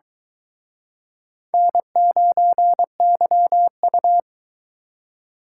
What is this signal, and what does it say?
Morse code, heard as a keyed tone, about 23 wpm, 700 Hz. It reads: N9YU